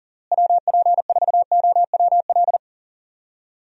Morse code, heard as a keyed tone, about 40 wpm, 700 Hz. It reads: WP4OWL
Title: USA Call Signs - 6 Characters - Repeated 2x - 40wpm